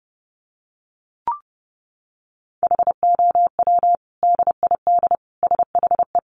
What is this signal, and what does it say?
Morse code, heard as a keyed tone, about 28 wpm, 700 Hz. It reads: HOW DID SHE
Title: Sentences from Top 100 Words 30wpm